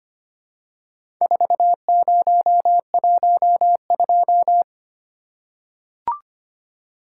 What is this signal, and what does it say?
Morse code, heard as a keyed tone, about 25 wpm, 700 Hz. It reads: 4012
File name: practice / 25wpm / Four Letter-Number Combinations - 25wpm